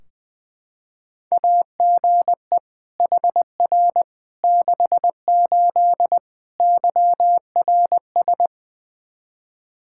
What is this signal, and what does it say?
Morse code, heard as a keyed tone, about 20 wpm, 700 Hz. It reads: AGE HR 68 YRS
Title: Practice QSO - Exchange 3 - More Info - 20wpm